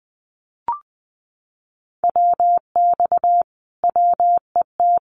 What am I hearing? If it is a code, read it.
Morse code, heard as a keyed tone, about 20 wpm, 700 Hz. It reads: WX WET
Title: Common QSO Phrases Encore 20wpm